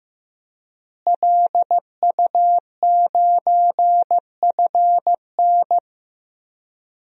Morse code, heard as a keyed tone, about 15 wpm, 700 Hz. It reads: LU9FN
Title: DXCC-100 Call Signs - 5 Characters - Repeated 3x - 15wpm